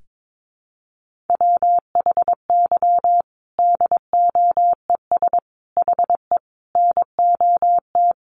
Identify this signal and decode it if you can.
Morse code, heard as a keyed tone, about 22 wpm, 700 Hz. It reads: WHY DOES HE NOT
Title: Sentences from Top 1400 Words 22wpm